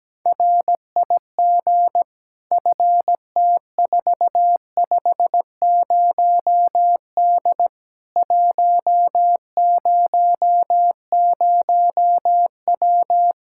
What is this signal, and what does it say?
Morse code, heard as a keyed tone, about 17 wpm, 700 Hz. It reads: RIG FT450D 100W
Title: Practice QSO - Exchange 2 - Info - 17wpm